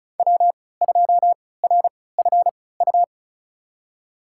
Morse code, heard as a keyed tone, about 36 wpm, spaced as for 20 wpm, 700 Hz. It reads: W2RFU
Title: USA Call Signs - 5 Characters - 3x Character Spacing - 35wpm